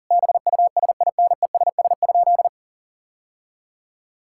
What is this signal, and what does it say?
Morse code, heard as a keyed tone, about 40 wpm, 700 Hz. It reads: BUSINESS?
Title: Sentences from Top 1000 Words - Review Part 6 - 40wpm